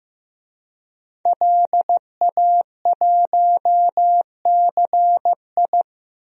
Morse code, heard as a keyed tone, about 15 wpm, 700 Hz. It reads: LA1CI
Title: DXCC-100 Call Signs - 5 Characters - 15wpm